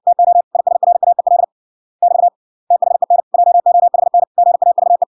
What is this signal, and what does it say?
Morse code, heard as a keyed tone, about 56 wpm, 700 Hz. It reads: TO ISRAEL <BT> THEN JOHN GAVE